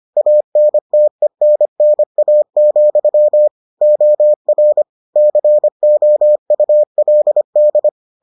Morse code, heard as a keyed tone, about 25 wpm, 600 Hz. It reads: ANTENNA, OR COULD